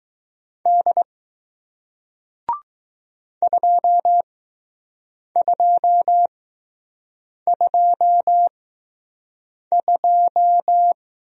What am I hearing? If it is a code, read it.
Morse code, heard as a keyed tone, about 20 wpm, spaced as for 6 wpm, 700 Hz. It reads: D 2222